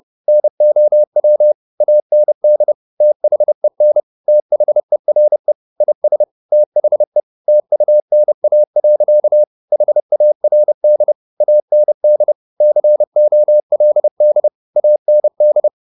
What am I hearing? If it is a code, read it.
Morse code, heard as a keyed tone, about 30 wpm, 600 Hz. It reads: NOW AND THEN THERE IS THE TUNA. HARD AND COLD AND
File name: the old man and the sea / the old man and the sea 1 30wpm